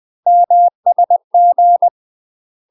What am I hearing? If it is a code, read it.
Morse code, heard as a keyed tone, about 20 wpm, 700 Hz. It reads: MSG
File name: UK MASTER QSOs TRAINING/20WPM